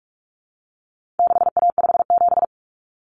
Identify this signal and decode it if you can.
Morse code, heard as a keyed tone, about 44 wpm, 700 Hz. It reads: 6A57
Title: Common Cut Numbers - 4-Digit 45wpm